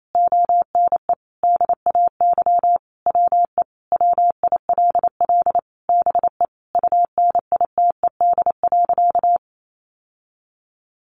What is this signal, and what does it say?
Morse code, heard as a keyed tone, about 28 wpm, 700 Hz. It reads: ONE DAY WE WILL BE UNITED.
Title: Sentences from Top 600 Words 28wpm